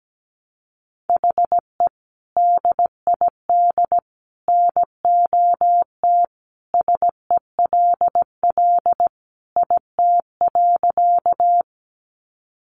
Morse code, heard as a keyed tone, about 17 wpm, 700 Hz. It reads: HE DID NOT SELL IT.
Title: Sentences from Top 1100 Words 17wpm